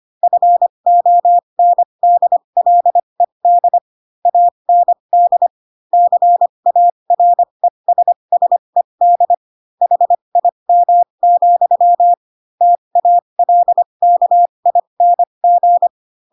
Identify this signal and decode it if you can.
Morse code, heard as a keyed tone, about 25 wpm, spaced as for 22 wpm, 700 Hz. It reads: FONDLED AND CARESSED HIM, TALKING
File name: Pinocchio / Pinocchio ch7 18 25 ik1hge